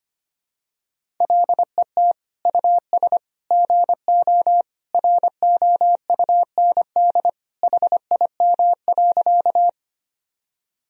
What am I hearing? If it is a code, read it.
Morse code, heard as a keyed tone, about 25 wpm, 700 Hz. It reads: LET US GO ROUND HIM.
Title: Sentences from Top 1000 Words - Review Part 1 - 25wpm